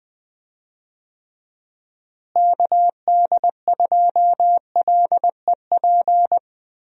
Morse code, heard as a keyed tone, about 21 wpm, 700 Hz. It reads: KD2LEP
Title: USA Call Signs - 6 Characters - Repeated 3x - 20wpm